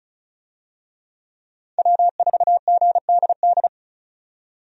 Morse code, heard as a keyed tone, about 36 wpm, 700 Hz. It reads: W4GDD
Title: USA Call Signs - 5 Characters - 35wpm